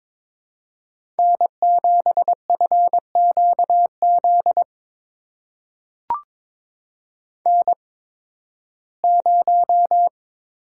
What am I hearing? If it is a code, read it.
Morse code, heard as a keyed tone, about 22 wpm, spaced as for 14 wpm, 700 Hz. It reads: N7FQZ N0
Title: USA Call Signs - 8x Character Spacing - 22wpm